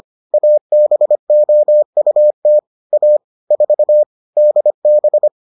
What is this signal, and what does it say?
Morse code, heard as a keyed tone, about 25 wpm, 600 Hz. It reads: ABOUT A 4 DB